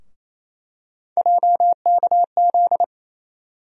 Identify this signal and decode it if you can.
Morse code, heard as a keyed tone, about 28 wpm, 700 Hz. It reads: JKZ
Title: USA Call Sign Postfixes - 3 Letters 28wpm